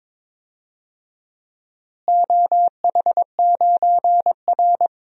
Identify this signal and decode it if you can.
Morse code, heard as a keyed tone, about 22 wpm, 700 Hz. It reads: OH9R
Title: DXCC-100 Call Signs - 4 Characters - Repeated 3x - 22wpm